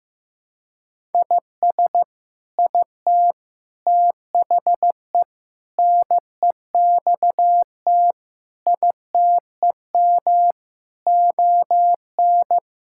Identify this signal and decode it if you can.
Morse code, heard as a keyed tone, about 15 wpm, 700 Hz. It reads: IS IT THE NEXT ITEM ON
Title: Sentences from Top 1000 Words - Review Part 6 - 15wpm